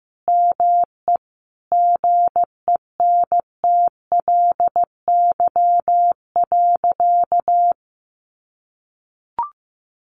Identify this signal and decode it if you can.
Morse code, heard as a keyed tone, about 15 wpm, 700 Hz. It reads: ME GENTLY.
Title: Sentences from Top 1500 Words 15wpm